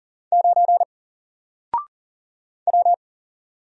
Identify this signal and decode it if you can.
Morse code, heard as a keyed tone, about 39 wpm, 700 Hz. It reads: 9 W
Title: Single Letter-Number Alt 40wpm